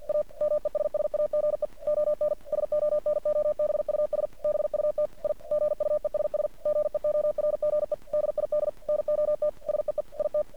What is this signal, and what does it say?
Morse code, heard as a keyed tone, about 50 wpm, 600 Hz. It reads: A MESSAGE ON SONOBUS BUT I GUESS GEORGE DID NOT SEE IT
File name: archived / CRSnet